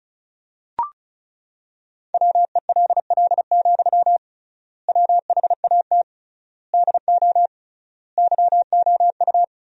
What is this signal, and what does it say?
Morse code, heard as a keyed tone, about 36 wpm, 700 Hz. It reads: WELL, WHAT DO YOU
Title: 3x Word-Spacing - Sentences from Top 200 Words - 35wpm